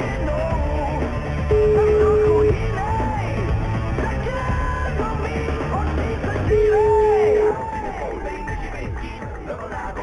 Ringback tone (Europe, 425 Hz), with noise.